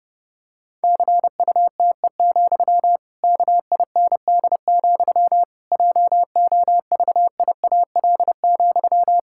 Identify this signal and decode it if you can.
Morse code, heard as a keyed tone, about 30 wpm, 700 Hz. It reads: CUTE, KIND, JOVIAL,
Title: Pangram Challenge v2 - 30wpm